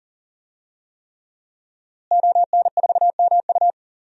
Morse code, heard as a keyed tone, about 40 wpm, 700 Hz. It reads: ON4MU
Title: DXCC-100 Call Signs - 5 Characters - Repeated 3x - 40wpm